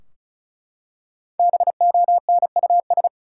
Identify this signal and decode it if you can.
Morse code, heard as a keyed tone, about 36 wpm, 700 Hz. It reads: BONUS
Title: Five Letter Words - 35wpm